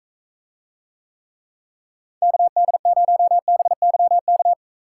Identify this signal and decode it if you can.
Morse code, heard as a keyed tone, about 44 wpm, 700 Hz. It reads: KD0BYK